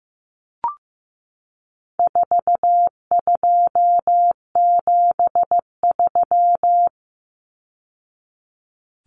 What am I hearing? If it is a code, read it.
Morse code, heard as a keyed tone, about 15 wpm, 700 Hz. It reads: 4273
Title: Numbers - 4-Digit 15wpm